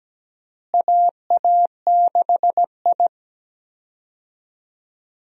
Morse code, heard as a keyed tone, about 17 wpm, 700 Hz. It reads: AA6I
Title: USA Call Signs - 4 Characters - Repeated 3x - 17wpm